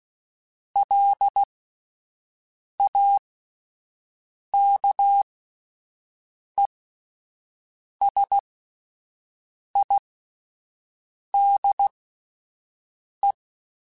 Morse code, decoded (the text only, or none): LAKESIDE